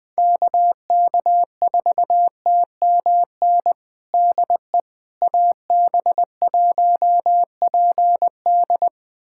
Morse code, heard as a keyed tone, about 21 wpm, 700 Hz. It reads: KK4TMN DE AB1PD
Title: USA Callsign DE Callsign 20wpm